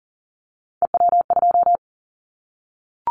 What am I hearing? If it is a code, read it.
Morse code, heard as a keyed tone, about 40 wpm, 700 Hz. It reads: EW2
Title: Callsign Prefixes for Top 100 DXCC - 40wpm